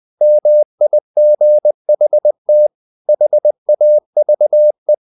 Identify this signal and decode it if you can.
Morse code, heard as a keyed tone, about 20 wpm, 600 Hz. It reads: MIGHT HAVE
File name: the old man and the sea part 2 20wpm